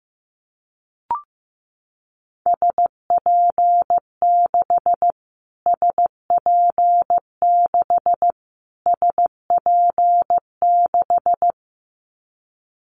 Morse code, heard as a keyed tone, about 15 wpm, 700 Hz. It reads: SP6 SP6 SP6
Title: Callsign Prefixes for Top 25 DXCC - Repeated 3x - 15wpm